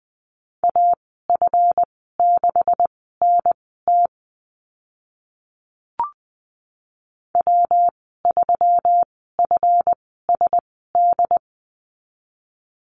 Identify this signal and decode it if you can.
Morse code, heard as a keyed tone, about 20 wpm, spaced as for 14 wpm, 700 Hz. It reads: AF6NT W3FSD
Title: 147 Lesson 24g - Letter y - Call Signs 20wpm